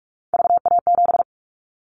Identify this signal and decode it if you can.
Morse code, heard as a keyed tone, about 46 wpm, 700 Hz. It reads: VA7
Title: Callsign Prefixes for Top 50 DXCC - 45wpm